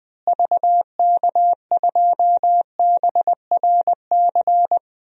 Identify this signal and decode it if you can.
Morse code, heard as a keyed tone, about 20 wpm, 700 Hz. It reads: VK2BRC
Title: DXCC-100 Call Signs - 6 Characters - Repeated 3x - 20wpm